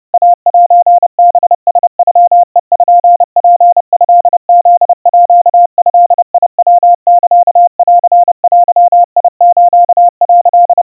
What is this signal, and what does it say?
Morse code (Japanese Wabun code), heard as a keyed tone, about 30 wpm, 700 Hz. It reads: イセハラノペツトフードヤサンデス」
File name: 110 cpm 700 Hz Crispy